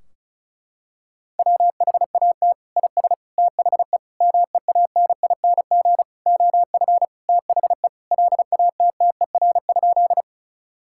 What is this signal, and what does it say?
Morse code, heard as a keyed tone, about 34 wpm, 700 Hz. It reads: WHAT IS THE MEANING OF THE LATTER?
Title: Sentences from Top 1600 Words 35wpm